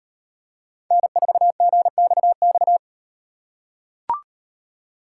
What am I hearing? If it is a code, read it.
Morse code, heard as a keyed tone, about 38 wpm, 700 Hz. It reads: N4GXX